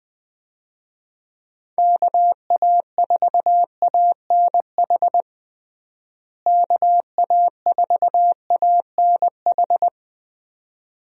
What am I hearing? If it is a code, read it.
Morse code, heard as a keyed tone, about 21 wpm, 700 Hz. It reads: KA4ANH KA4ANH
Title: USA Call Signs - 6 Characters - Repeated 2x - 20wpm